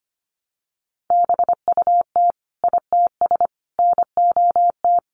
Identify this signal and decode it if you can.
Morse code, heard as a keyed tone, about 25 wpm, 700 Hz. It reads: BUT ITS NOT
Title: Sentences from Top 600 Words 25wpm